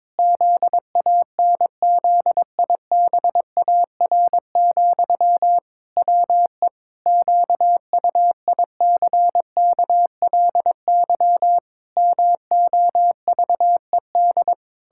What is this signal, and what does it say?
Morse code, heard as a keyed tone, about 22 wpm, 700 Hz. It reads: ZANZIBAR, WE QUICKLY MOVED
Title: Pangram Challenge v2 - 22wpm